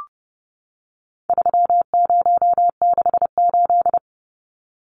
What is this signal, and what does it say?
Morse code, heard as a keyed tone, about 30 wpm, 700 Hz. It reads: 3068
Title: Numbers - 4-Digit 30wpm